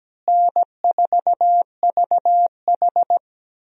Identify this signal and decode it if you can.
Morse code, heard as a keyed tone, about 17 wpm, 700 Hz. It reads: N4VH
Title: USA Call Signs - 4 Characters - Repeated 3x - 17wpm